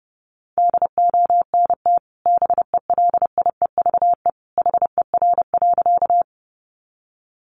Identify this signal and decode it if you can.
Morse code, heard as a keyed tone, about 30 wpm, 700 Hz. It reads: DONT BELIEVE HER.
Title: Sentences from Top 1000 Words 30wpm